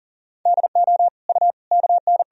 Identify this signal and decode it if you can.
Morse code, heard as a keyed tone, about 40 wpm, 700 Hz. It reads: DO U KN
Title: Sentences from Top 100 Words and Abbreviations 40wpm